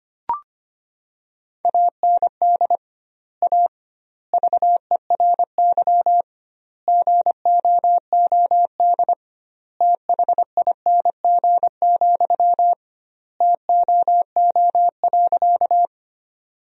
Morse code, heard as a keyed tone, about 25 wpm, 700 Hz. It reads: AND A VERY GOOD THING, TOO.
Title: 2x Word-Spacing - Sentences from Top 200 Words - 25wpm